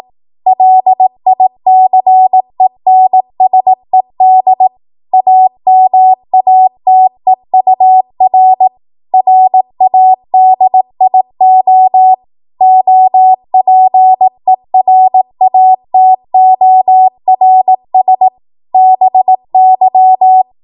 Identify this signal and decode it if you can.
Morse code, heard as a keyed tone, about 18 wpm, 750 Hz. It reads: LICENSED AMATEUR RADIO OPERATORS BY
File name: Morse / 18WPM